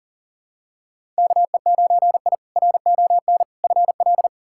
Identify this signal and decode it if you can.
Morse code, heard as a keyed tone, about 40 wpm, 700 Hz. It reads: KE9I RON FL
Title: K1USN SST QSO Practice - 40wpm